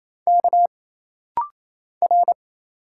Morse code, heard as a keyed tone, about 29 wpm, 700 Hz. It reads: K R
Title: Single Letters - Rapid-Fire 28wpm